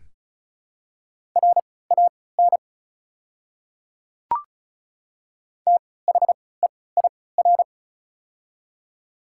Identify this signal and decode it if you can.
Morse code, heard as a keyed tone, about 36 wpm, spaced as for 20 wpm, 690 Hz. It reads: RAN THEIR
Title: 3x Character Spacing - Top 500 Words - 35wpm